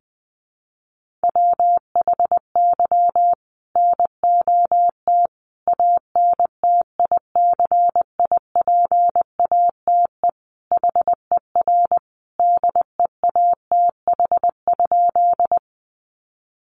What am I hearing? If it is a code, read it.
Morse code, heard as a keyed tone, about 20 wpm, 700 Hz. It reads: WHY NOT ANTICIPATE HER DEATH?
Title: Sentences from Top 1500 Words 20wpm